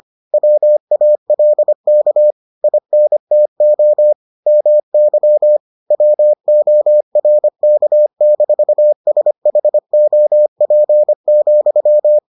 Morse code, heard as a keyed tone, about 25 wpm, 600 Hz. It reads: WALK INTO MY WORK-SHOP,